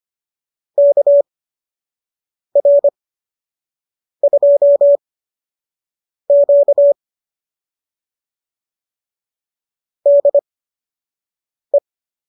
Morse code, heard as a keyed tone, about 25 wpm, spaced as for 6 wpm, 570 Hz. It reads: KR2Q DE